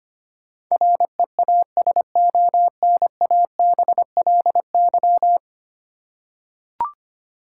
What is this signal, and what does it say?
Morse code, heard as a keyed tone, about 25 wpm, 700 Hz. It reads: REASONABLY
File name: 10 Letter Words - First 3 Letters Given - 25wpm